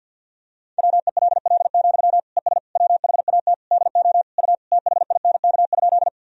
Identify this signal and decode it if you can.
Morse code, heard as a keyed tone, about 51 wpm, 700 Hz. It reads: WELL, ES WHAT DO U THINK?
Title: Sentences from Top 200 Words and Abbreviations 50wpm